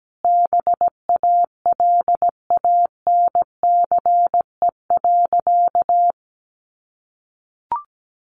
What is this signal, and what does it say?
Morse code, heard as a keyed tone, about 17 wpm, 700 Hz. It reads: BALANCE.
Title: A Sound Of Thunder 17wpm